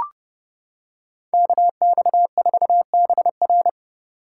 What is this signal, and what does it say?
Morse code, heard as a keyed tone, about 28 wpm, 700 Hz. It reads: KX4BR